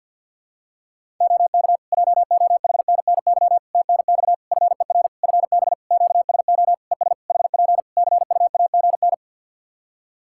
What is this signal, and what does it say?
Morse code, heard as a keyed tone, about 48 wpm, 700 Hz. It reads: OK JOHNNY TNX FER FB QSO ES HP CUAGN
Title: Practice QSO - Exchange 4 - Ending - 50wpm